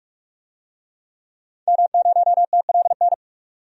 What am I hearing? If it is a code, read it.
Morse code, heard as a keyed tone, about 47 wpm, 700 Hz. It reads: M0TLN
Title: DXCC-100 Call Signs - 5 Characters - 45wpm